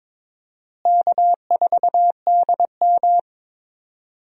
Morse code, heard as a keyed tone, about 22 wpm, 700 Hz. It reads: K4DM
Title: USA Call Signs - 4 Characters - Repeated 2x - 22wpm